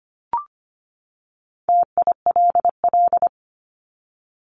Morse code, heard as a keyed tone, about 25 wpm, 700 Hz. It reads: TILL
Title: Top 1751-2000 Words 25wpm